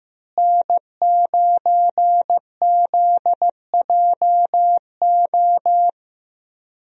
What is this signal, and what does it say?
Morse code, heard as a keyed tone, about 14 wpm, 700 Hz. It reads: N9ZJO
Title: USA Call Signs - 5 Characters - Repeated 3x - 15wpm